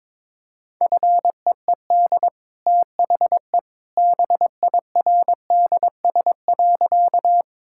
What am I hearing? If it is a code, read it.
Morse code, heard as a keyed tone, about 22 wpm, 700 Hz. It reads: FEED THE BIRDS.
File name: practice / 22wpm / Easy Sentences with Hints - 22wpm